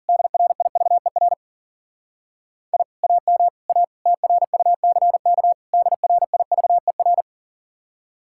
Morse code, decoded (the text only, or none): DRIVER I AM A TRUCK DRIVER